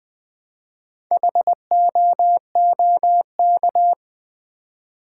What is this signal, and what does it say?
Morse code, heard as a keyed tone, about 20 wpm, 700 Hz. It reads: HOOK
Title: Build A Word - 1 Letter At A Time - 4-Letter Words - 20wpm